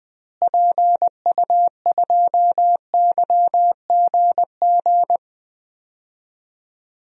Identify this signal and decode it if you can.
Morse code, heard as a keyed tone, about 21 wpm, 700 Hz. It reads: PU2YGG
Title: DXCC-100 Call Signs - 6 Characters - Repeated 3x - 20wpm